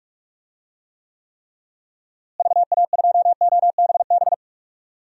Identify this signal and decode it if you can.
Morse code, heard as a keyed tone, about 46 wpm, 700 Hz. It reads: VA2OBB
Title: DXCC-100 Call Signs - 6 Characters - Repeated 3x - 45wpm